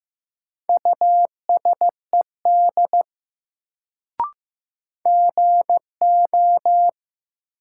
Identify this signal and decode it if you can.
Morse code, heard as a keyed tone, about 16 wpm, 700 Hz. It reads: USED GO
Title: Sets of 2 Words - New Word Drill - 15wpm